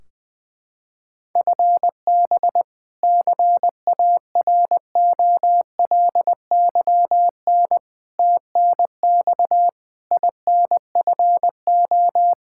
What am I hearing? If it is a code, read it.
Morse code, heard as a keyed tone, about 21 wpm, 700 Hz. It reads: FB CAROLYN TNX INFO
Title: Practice QSO - Exchange 3 - More Info - 20wpm